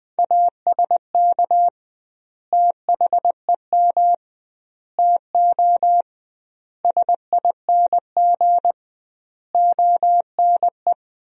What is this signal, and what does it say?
Morse code, heard as a keyed tone, about 20 wpm, 700 Hz. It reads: ASK THEM TO SING ONE
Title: 2x Word-Spacing - Sentences from Top 400 Words - 20wpm